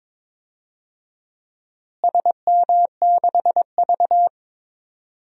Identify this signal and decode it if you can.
Morse code, heard as a keyed tone, about 22 wpm, 700 Hz. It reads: SM6V
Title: DXCC-100 Call Signs - 4 Characters - Repeated 3x - 22wpm